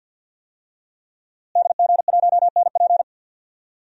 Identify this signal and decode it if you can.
Morse code, heard as a keyed tone, about 53 wpm, 700 Hz. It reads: DG1NP
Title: DXCC-100 Call Signs - 5 Characters - Repeated 2x - 50wpm